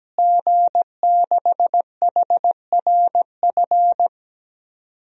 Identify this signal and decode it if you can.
Morse code, heard as a keyed tone, about 17 wpm, 700 Hz. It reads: G6HRF
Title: DXCC-100 Call Signs - 5 Characters - Repeated 2x - 17wpm